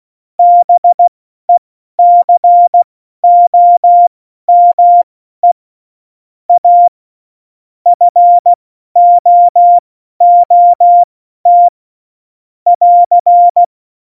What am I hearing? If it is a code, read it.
Morse code, heard as a keyed tone, about 16 wpm, spaced as for 12 wpm, 700 Hz. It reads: BECOME A FOOT <AR>
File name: Morsekurs / Witze